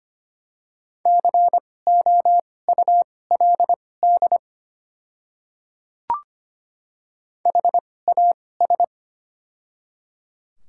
Morse code, heard as a keyed tone, about 25 wpm, spaced as for 18 wpm, 700 Hz. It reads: COULD HAS
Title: 2x Character Spacing - Top 100 Words - 25wpm